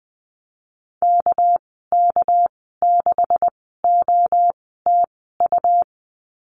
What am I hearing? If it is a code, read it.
Morse code, heard as a keyed tone, about 20 wpm, spaced as for 14 wpm, 700 Hz. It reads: KK6OTU
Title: 245 Lesson 39g - Letter z - Call Signs 20wpm